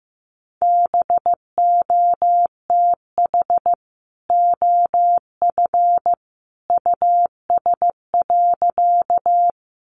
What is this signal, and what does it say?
Morse code, heard as a keyed tone, about 15 wpm, 700 Hz. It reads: BOTH OF US.
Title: Sentences from Top 300 Words 15wpm